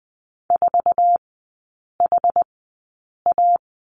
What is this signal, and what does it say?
Morse code, heard as a keyed tone, about 20 wpm, spaced as for 8 wpm, 700 Hz. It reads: 4HA
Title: 107 Lesson 18e - Letter m - Three Characters 20wpm